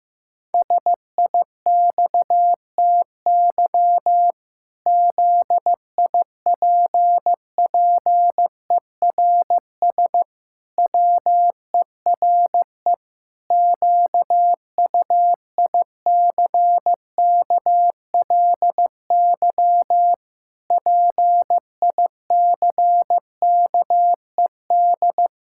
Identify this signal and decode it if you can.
Morse code, heard as a keyed tone, about 15 wpm, 700 Hz. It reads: SIXTY ZIPPERS WERE QUICKLY PICKED
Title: Pangram Challenge v2 - 15wpm